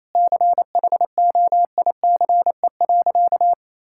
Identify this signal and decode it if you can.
Morse code, heard as a keyed tone, about 28 wpm, 700 Hz. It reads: CHOICE.
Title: Sentences from Top 1000 Words - Review Part 6 - 28wpm